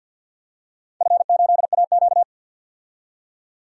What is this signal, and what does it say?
Morse code, heard as a keyed tone, about 53 wpm, 700 Hz. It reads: F8AQ
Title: DXCC-100 Call Signs - 4 Characters - 50wpm